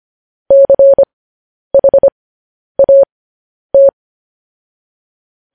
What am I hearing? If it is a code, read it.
Morse code, heard as a keyed tone, about 25 wpm, spaced as for 10 wpm, 550 Hz. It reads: CHAT
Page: news-headlines-in-morse-code-at-10-wpm